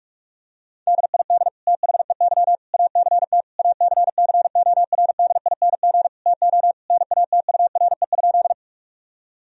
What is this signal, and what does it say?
Morse code, heard as a keyed tone, about 46 wpm, 700 Hz. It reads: DID THEY ACT ACCORDING TO NATURE?
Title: Sentences from Top 1000 Words - Review Part 4 - 45wpm